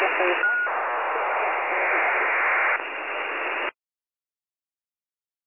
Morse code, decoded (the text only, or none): none